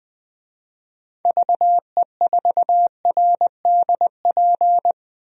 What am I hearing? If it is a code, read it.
Morse code, heard as a keyed tone, about 20 wpm, 700 Hz. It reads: VE4RDP
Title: DXCC-100 Call Signs - 6 Characters - 20wpm